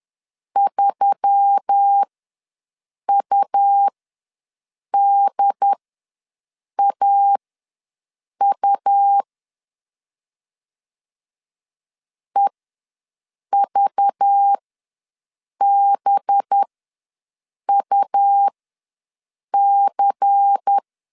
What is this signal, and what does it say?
Morse code, heard as a keyed tone, about 10 wpm, spaced as for 6 wpm, 800 Hz. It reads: 3UDAU EVBUC
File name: LEZIONI / 5 LBVC / 5 LBVC ws 10wpm 3733